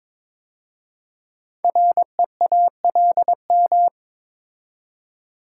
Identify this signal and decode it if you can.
Morse code, heard as a keyed tone, about 22 wpm, 700 Hz. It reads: REALM
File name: 5 Letter Words - First 3 Letters Given - 22wpm